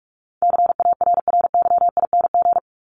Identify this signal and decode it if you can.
Morse code, heard as a keyed tone, about 44 wpm, 700 Hz. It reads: CARRYING
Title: ING Suffix - Top 100 Words with Base Words 45wpm